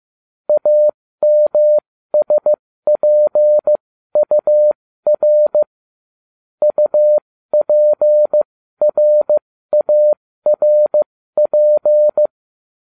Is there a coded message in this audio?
Yes: Morse code from 0.5 to 12.3 s, heard as a keyed tone, about 15 wpm, spaced as for 13 wpm, 600 Hz. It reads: AMSPUR UPRARP